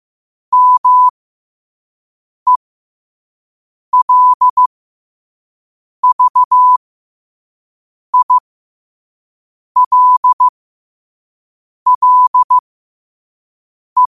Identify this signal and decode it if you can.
Morse code, heard as a keyed tone, about 15 wpm, spaced as for 5 wpm, 1000 Hz. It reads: MELVILLE